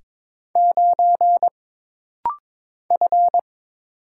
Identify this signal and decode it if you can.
Morse code, heard as a keyed tone, about 22 wpm, 700 Hz. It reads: 9 F
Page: This rapid-fire timing is geared to push your ICR (Instant Character Recognition) skills.